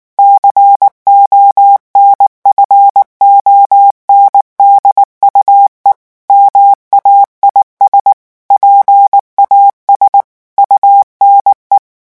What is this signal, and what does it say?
Morse code, heard as a keyed tone, about 19 wpm, 800 Hz. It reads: CONFONDUE MAIS PAS UNE
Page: REVISION 8 - 8 mots minute